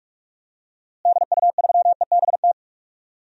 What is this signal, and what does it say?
Morse code, heard as a keyed tone, about 42 wpm, 700 Hz. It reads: DU3EBT
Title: DXCC-100 Call Signs - 6 Characters - Repeated 2x - 45wpm